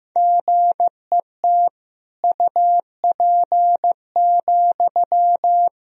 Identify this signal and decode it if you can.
Morse code, heard as a keyed tone, about 15 wpm, 700 Hz. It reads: GET UP,
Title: Sentences from Top 1000 Words - Review Part 1 - 15wpm